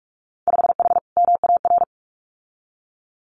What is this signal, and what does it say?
Morse code, heard as a keyed tone, about 44 wpm, 700 Hz. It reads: 5H MAR
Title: Winter Field Day Exchanges Encore 45wpm